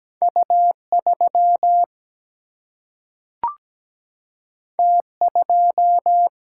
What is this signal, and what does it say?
Morse code, heard as a keyed tone, about 17 wpm, 700 Hz. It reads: U 3 T 2
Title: Single-Letter Single-Number 17wpm